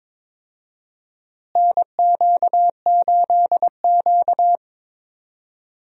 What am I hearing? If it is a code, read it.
Morse code, heard as a keyed tone, about 22 wpm, 700 Hz. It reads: NQ8Q
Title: USA Call Signs - 4 Characters - Repeated 3x - 22wpm